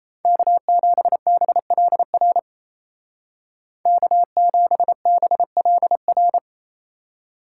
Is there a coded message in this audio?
Yes: Morse code from 0.2 to 6.4 s, heard as a keyed tone, about 37 wpm, 700 Hz. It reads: K7BLR K7BLR